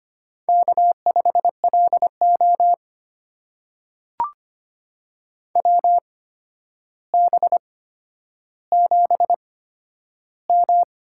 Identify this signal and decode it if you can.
Morse code, heard as a keyed tone, about 25 wpm, spaced as for 11 wpm, 700 Hz. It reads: K5LO WB7M
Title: USA Call Signs - 4 Characters - 8x Character Spacing - 25wpm